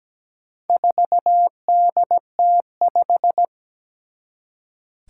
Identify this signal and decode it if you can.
Morse code, heard as a keyed tone, about 17 wpm, 700 Hz. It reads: 4DT5
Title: Four Challenging Letter-Number Combinations 17wpm